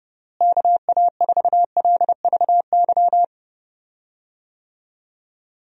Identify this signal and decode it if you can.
Morse code, heard as a keyed tone, about 32 wpm, 700 Hz. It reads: KA4LVY
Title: USA Call Signs - 6 Characters - Repeated 2x - 30wpm